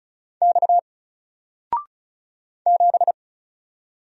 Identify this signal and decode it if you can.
Morse code, heard as a keyed tone, about 34 wpm, 700 Hz. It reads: X 7
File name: Single-Letter-Number-35wpm